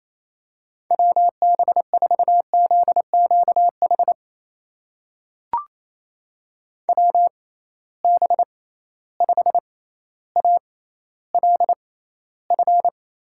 Morse code, heard as a keyed tone, about 28 wpm, spaced as for 14 wpm, 700 Hz. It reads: WB4ZQH WB5ALF
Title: USA Call Signs - 6 Characters - 6x Character Spacing - 28wpm